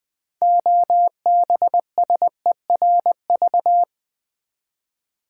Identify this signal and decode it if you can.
Morse code, heard as a keyed tone, about 20 wpm, 700 Hz. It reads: OBSERV
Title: Build A Word - 3 Letters At A Time - 8-Letter Words - 20wpm